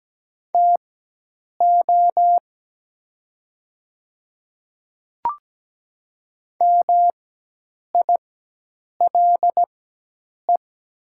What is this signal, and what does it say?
Morse code, heard as a keyed tone, about 17 wpm, spaced as for 8 wpm, 700 Hz. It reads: TO MILE
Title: 4x Character Spacing - Top 300 Words - 17wpm